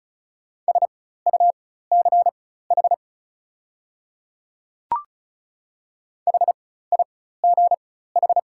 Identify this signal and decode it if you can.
Morse code, heard as a keyed tone, about 34 wpm, spaced as for 16 wpm, 690 Hz. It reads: SUCH HIGH
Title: 4x Character Spacing - Top 200 Words - 35wpm